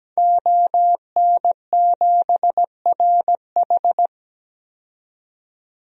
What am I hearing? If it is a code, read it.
Morse code, heard as a keyed tone, about 17 wpm, 700 Hz. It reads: ON7RH
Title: DXCC-100 Call Signs - 5 Characters - 17wpm